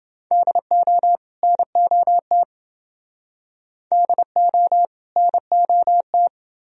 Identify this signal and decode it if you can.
Morse code, heard as a keyed tone, about 28 wpm, 700 Hz. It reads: DO NOT DO NOT